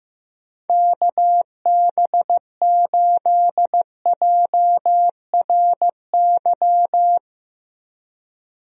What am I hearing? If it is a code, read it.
Morse code, heard as a keyed tone, about 15 wpm, 700 Hz. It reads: KB8JRY
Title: USA Call Signs - 6 Characters - Repeated 2x - 15wpm